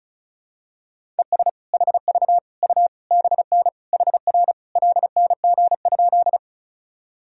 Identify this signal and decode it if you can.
Morse code, heard as a keyed tone, about 36 wpm, 700 Hz. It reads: ES HV U BN HR LNG?
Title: Sentences from Top 200 Words and Abbreviations 35wpm